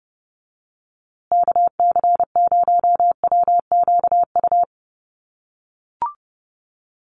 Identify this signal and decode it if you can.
Morse code, heard as a keyed tone, about 30 wpm, 700 Hz. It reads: KC0WQU
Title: USA Call Signs Repeated 2x 30wpm